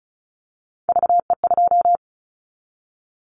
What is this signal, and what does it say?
Morse code, heard as a keyed tone, about 34 wpm, 700 Hz. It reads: VE2
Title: Callsign Prefixes for Top 25 DXCC - Repeated 3x - 35wpm